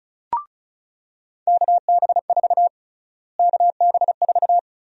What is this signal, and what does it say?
Morse code, heard as a keyed tone, about 34 wpm, 700 Hz. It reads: KB4 KB4
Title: USA Call Sign Prefixes - Repeated 2x - 35wpm